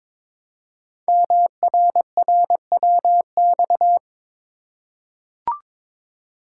Morse code, heard as a keyed tone, about 23 wpm, 700 Hz. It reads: MRRWX
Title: Code Groups - 5 Letters - 22wpm